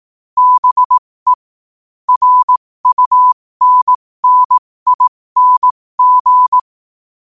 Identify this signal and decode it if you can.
Morse code, heard as a keyed tone, about 18 wpm, spaced as for 15 wpm, 1000 Hz. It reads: BE RUNNING
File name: qotdfiles / 15wpm